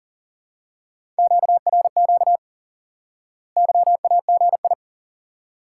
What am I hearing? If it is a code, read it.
Morse code, heard as a keyed tone, about 40 wpm, 700 Hz. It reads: QRQ YAGI
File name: IQE Course 3 - Lesson 89 - YAGI - Two Elements - 40wpm